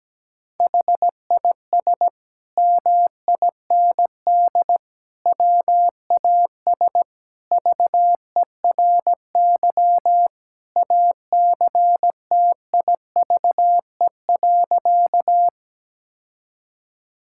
Morse code, heard as a keyed tone, about 17 wpm, 700 Hz. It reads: HIS MIND WAS VERY ACTIVE.
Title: Sentences from Top 1000 Words - Review Part 6 - 17wpm